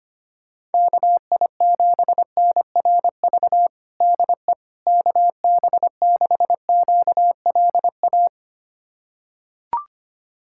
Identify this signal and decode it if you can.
Morse code, heard as a keyed tone, about 25 wpm, 700 Hz. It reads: KI7NRV DE KB6QLA
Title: USA Callsign DE Callsign 25wpm